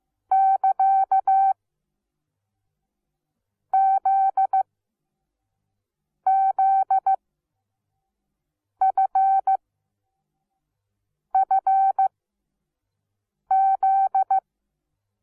Morse code, heard as a keyed tone, about 15 wpm, spaced as for 5 wpm, 770 Hz. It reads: <KA> ZZFFZ
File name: Onlinelehrgang / morsen / DJ4UF Morsen17